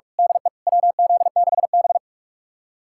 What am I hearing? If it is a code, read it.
Morse code, heard as a keyed tone, about 46 wpm, 700 Hz. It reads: DE WZ6B
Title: USA Callsign DE Callsign - 4 Characters - 45wpm